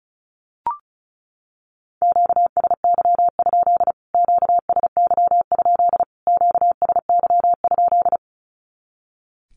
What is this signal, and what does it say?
Morse code, heard as a keyed tone, about 34 wpm, 700 Hz. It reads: QSY? QSY? QSY?
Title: Q-Codes Repeated 3x 35wpm